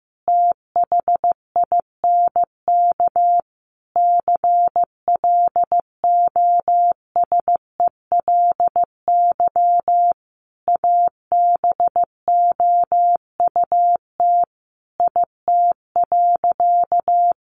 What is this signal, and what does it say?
Morse code, heard as a keyed tone, about 14 wpm, 700 Hz. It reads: THINK CLOSELY ABOUT IT.
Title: Sentences from Top 1400 Words 15wpm